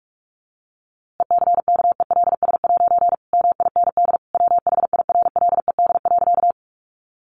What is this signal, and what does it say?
Morse code, heard as a keyed tone, about 44 wpm, 700 Hz. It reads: ECKELS' MIND WHIRLED.
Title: A Sound Of Thunder 45wpm